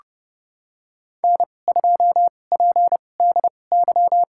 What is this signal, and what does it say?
Morse code, heard as a keyed tone, about 30 wpm, spaced as for 21 wpm, 700 Hz. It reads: N2PDY
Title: USA Call Signs - 5 Characters - 2x Character Spacing - 30wpm